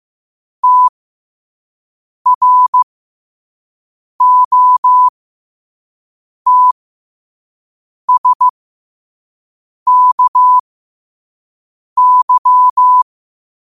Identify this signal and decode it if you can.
Morse code, heard as a keyed tone, about 15 wpm, spaced as for 5 wpm, 1000 Hz. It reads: TROTSKY
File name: qotdfiles / 5wpm